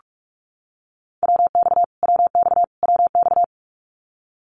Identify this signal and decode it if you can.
Morse code, heard as a keyed tone, about 44 wpm, 700 Hz. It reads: WX WX WX
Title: Abbreviations Repeated 3x - 45wpm